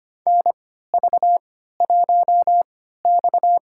Morse code, heard as a keyed tone, about 25 wpm, spaced as for 14 wpm, 700 Hz. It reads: NV1X
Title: USA Call Signs - 4 Characters - 3x Character Spacing - 25wpm